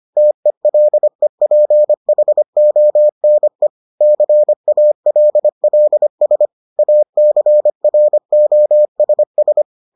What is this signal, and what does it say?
Morse code, heard as a keyed tone, about 25 wpm, 600 Hz. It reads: TELEPHONE CALLS ACROSS